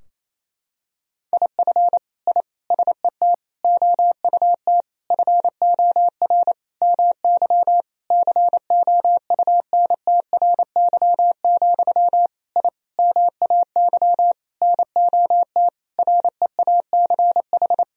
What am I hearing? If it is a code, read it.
Morse code, heard as a keyed tone, about 28 wpm, 700 Hz. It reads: IF I SET OUT FOR MY COUNTRY, I MAY NOT REACH
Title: Sentences from Top 1000 Words - Review Part 2 - 28wpm